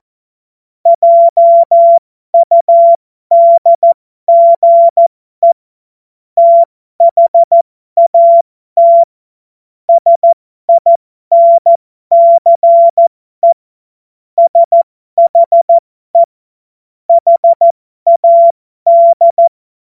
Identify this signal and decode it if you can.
Morse code, heard as a keyed tone, about 14 wpm, spaced as for 12 wpm, 680 Hz. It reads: JUDGE THAT SINCE SHE HAD
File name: Morsekurs / Witze